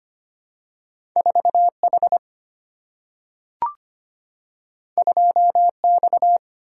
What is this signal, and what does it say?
Morse code, heard as a keyed tone, about 25 wpm, 690 Hz. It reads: 4H 2X
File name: Two Letter-Number Combinations - 25wpm